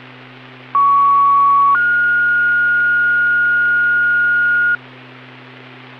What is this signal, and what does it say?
Two-tone page, heard as a one-second tone, then a three-second tone, about 1120 then 1430 Hz, with noise. It is heard from 0.7 to 4.8 s.